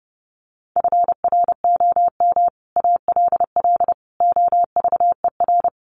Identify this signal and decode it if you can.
Morse code, heard as a keyed tone, about 30 wpm, 700 Hz. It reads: FROM ALL OVER
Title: Sets of 5 Words 30wpm